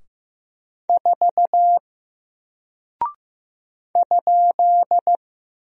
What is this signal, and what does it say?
Morse code, heard as a keyed tone, about 15 wpm, 700 Hz. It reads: 4 ?
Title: Single Character (Letter:Number:Punctuation:Prosign) 15wpm